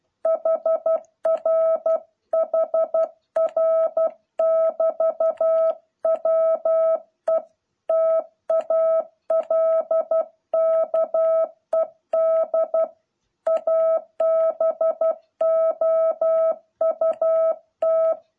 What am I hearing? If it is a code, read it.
Morse code, heard as a keyed tone, about 12 wpm, 660 Hz. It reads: HR HR <BT> WE TALKED ABOUT